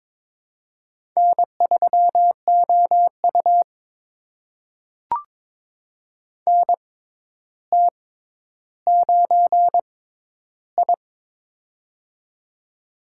Morse code, heard as a keyed tone, about 22 wpm, spaced as for 11 wpm, 700 Hz. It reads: N3OU NT9I
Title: USA Call Signs - 4 Characters - 6x Character Spacing - 22wpm